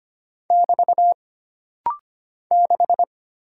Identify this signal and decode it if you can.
Morse code, heard as a keyed tone, about 25 wpm, 700 Hz. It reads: <BT> 6
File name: Single Character (Letter:Number:Punctuation:Prosign) - Rapid-Fire - 25wpm